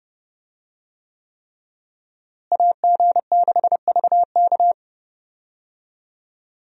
Morse code, heard as a keyed tone, about 30 wpm, 700 Hz. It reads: AG6VK
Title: USA Call Signs - 5 Characters - Repeated 3x - 30wpm